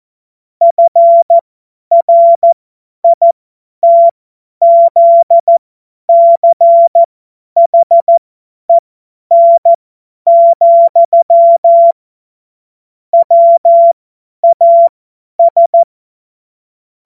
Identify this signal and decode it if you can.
Morse code, heard as a keyed tone, about 14 wpm, spaced as for 10 wpm, 680 Hz. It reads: FRITZCHEN, WAS
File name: Morsekurs / Witze / Witz 14. März 2023 50 F70